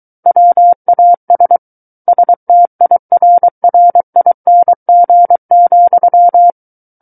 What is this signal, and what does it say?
Morse code, heard as a keyed tone, about 23 wpm, 700 Hz. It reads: WAS STIRRING,
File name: Book One / 14-Chapter 14 - 23 WPM